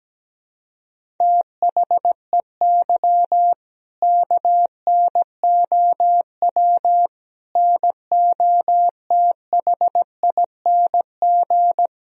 Morse code, heard as a keyed tone, about 17 wpm, 700 Hz. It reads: THEY KNOW NOTHING
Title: Sentences from Top 500 Words and Abbreviations 17wpm